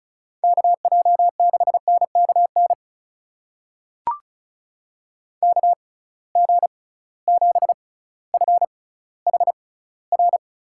Morse code, heard as a keyed tone, about 36 wpm, spaced as for 18 wpm, 700 Hz. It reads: KJ6NKN KG7FHR
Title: USA Call Signs - 6x Character Spacing - 35wpm